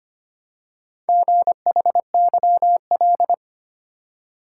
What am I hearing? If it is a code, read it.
Morse code, heard as a keyed tone, about 25 wpm, 700 Hz. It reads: GHYL
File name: practice / 25wpm / Four Letter-Number Combinations - 25wpm